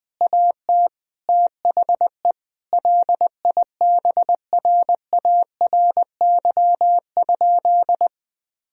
Morse code, heard as a keyed tone, about 20 wpm, 700 Hz. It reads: AT THE LIBRARY?
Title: Sentences from Top 1000 Words - Review Part 4 - 20wpm